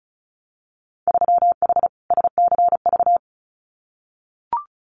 Morse code, heard as a keyed tone, about 36 wpm, 700 Hz. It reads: 3H SCV
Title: Winter Field Day Exchanges Encore 35wpm